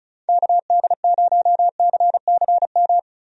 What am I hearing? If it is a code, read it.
Morse code, heard as a keyed tone, about 34 wpm, 700 Hz. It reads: KD0CCM
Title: USA Call Signs - 6 Characters - Repeated 2x - 35wpm